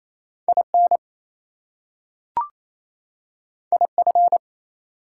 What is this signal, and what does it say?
Morse code, heard as a keyed tone, about 28 wpm, 700 Hz. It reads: IN IF
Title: Two Letter Words - 28wpm